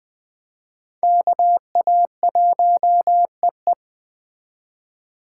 Morse code, heard as a keyed tone, about 21 wpm, 700 Hz. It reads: KA1EE
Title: Speed Racing - USA 2x2 Call Signs 17wpm